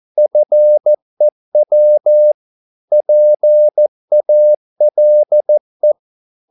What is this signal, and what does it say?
Morse code, heard as a keyed tone, about 14 wpm, 600 Hz. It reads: FEW PALE